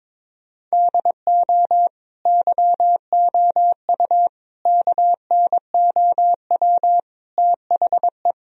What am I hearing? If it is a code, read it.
Morse code, heard as a keyed tone, about 22 wpm, 700 Hz. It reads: DO YOU KNOW THE
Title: Sentences from Top 1000 Words - Review Part 4 - 22wpm